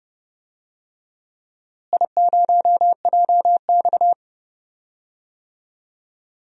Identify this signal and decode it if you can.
Morse code, heard as a keyed tone, about 30 wpm, 700 Hz. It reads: I0JX
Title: DXCC-100 Call Signs - 4 Characters - Repeated 3x - 30wpm